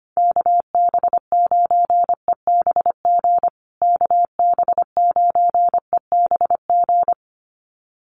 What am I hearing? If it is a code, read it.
Morse code, heard as a keyed tone, about 25 wpm, 700 Hz. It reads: KB9EBG KB9EBG
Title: USA Call Signs Repeated 2x 25wpm